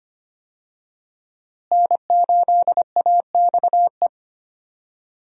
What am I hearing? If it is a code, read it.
Morse code, heard as a keyed tone, about 25 wpm, 700 Hz. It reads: N8AXE
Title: USA Call Signs - 5 Characters - Repeated 3x - 25wpm